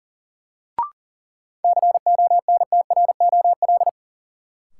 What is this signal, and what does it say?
Morse code, heard as a keyed tone, about 38 wpm, 700 Hz. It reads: CONTROL
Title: Sentences from Top 1000 Words - Review Part 2 - New Word Drill - 40wpm